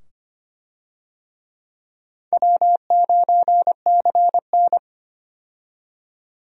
Morse code, heard as a keyed tone, about 25 wpm, 700 Hz. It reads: W9CN
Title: USA Call Signs - 4 Characters - 25wpm